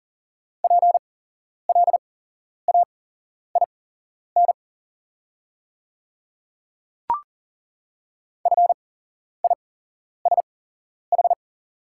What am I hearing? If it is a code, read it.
Morse code, heard as a keyed tone, about 38 wpm, spaced as for 11 wpm, 690 Hz. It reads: PLAIN FISH
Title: 8x Character Spacing - Top 500 Words - 40wpm